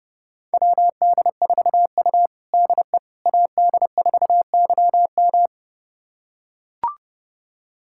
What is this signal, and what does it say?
Morse code, heard as a keyed tone, about 32 wpm, 700 Hz. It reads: WD4U DE AD4YM
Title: USA Callsign DE Callsign 30wpm